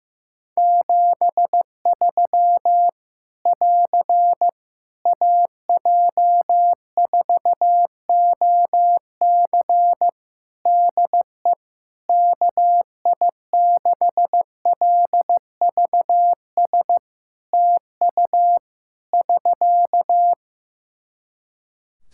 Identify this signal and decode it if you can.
Morse code, heard as a keyed tone, about 15 wpm, 700 Hz. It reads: 73 <AR> AJ4OC DE KI6LVS TU <SK>
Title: Practice QSO - Exchange 4 - Ending - 15wpm